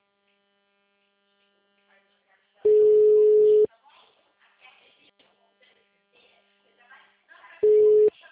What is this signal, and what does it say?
Ringback tone (Europe, 425 Hz), with speech.